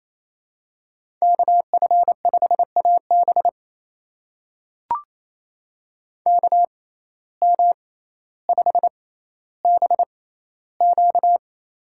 Morse code, heard as a keyed tone, about 28 wpm, spaced as for 14 wpm, 700 Hz. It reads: KF5AB KM5BQ
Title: USA Call Signs - 5 Characters - 6x Character Spacing - 28wpm